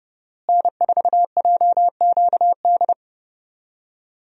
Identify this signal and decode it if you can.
Morse code, heard as a keyed tone, about 32 wpm, 700 Hz. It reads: N4JQD
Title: USA Call Signs - 5 Characters - Repeated 3x - 30wpm